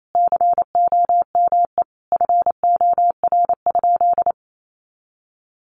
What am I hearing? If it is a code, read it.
Morse code, heard as a keyed tone, about 28 wpm, 700 Hz. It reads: COME FOR?
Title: Sentences from Top 100 Words 28wpm